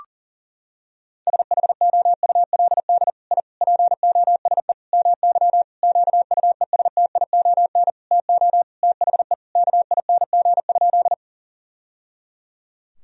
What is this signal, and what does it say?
Morse code, heard as a keyed tone, about 40 wpm, 700 Hz. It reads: SHOULD I POSE MY QUESTION TO THE KING?
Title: Sentences from Top 1000 Words - Review Part 2 - 40wpm